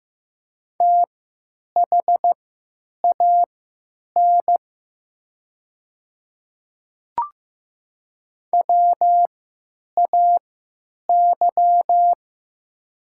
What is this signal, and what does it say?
Morse code, heard as a keyed tone, about 15 wpm, spaced as for 9 wpm, 700 Hz. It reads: THAN WAY
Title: 3x Character Spacing - Top 100 Words - 15wpm